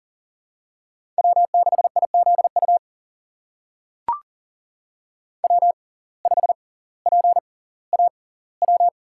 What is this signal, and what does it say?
Morse code, heard as a keyed tone, about 40 wpm, spaced as for 20 wpm, 700 Hz. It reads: W6IZU W5PAW
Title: USA Call Signs - 5 Characters - 6x Character Spacing - 40wpm